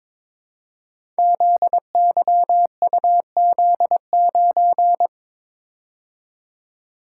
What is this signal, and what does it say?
Morse code, heard as a keyed tone, about 21 wpm, 700 Hz. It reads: ZYUZ9
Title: Code Groups - 5 Letter:Numbers - 22wpm